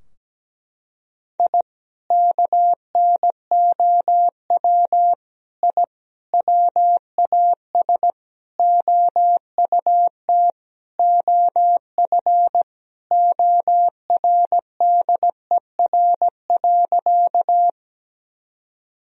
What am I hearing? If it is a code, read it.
Morse code, heard as a keyed tone, about 17 wpm, 700 Hz. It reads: I KNOW I WAS OUT OF ORDER.
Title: Sentences from Top 1000 Words - Review Part 2 - 17wpm